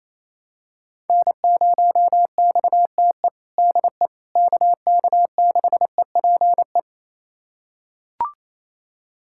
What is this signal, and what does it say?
Morse code, heard as a keyed tone, about 28 wpm, 700 Hz. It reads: N0XTE DE KK6EPE
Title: USA Callsign DE Callsign 28wpm